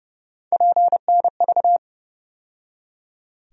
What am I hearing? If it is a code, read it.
Morse code, heard as a keyed tone, about 30 wpm, 700 Hz. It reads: PNV
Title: USA Call Sign Postfixes - 3 Letters 30wpm